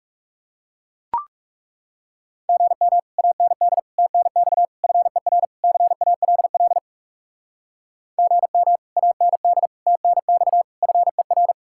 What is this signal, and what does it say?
Morse code, heard as a keyed tone, about 37 wpm, 700 Hz. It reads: GM AND TNX FER CALL GM AND TNX FER
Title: Speed-Racing Common QSO Phrases - 30wpm